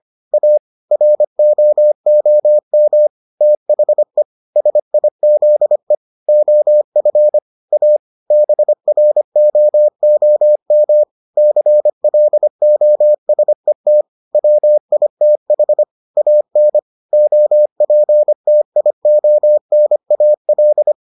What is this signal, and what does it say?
Morse code, heard as a keyed tone, about 25 wpm, 600 Hz. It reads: A ROOM THE SIZE OF A BROOM CLOSET WITH AN OPTIONAL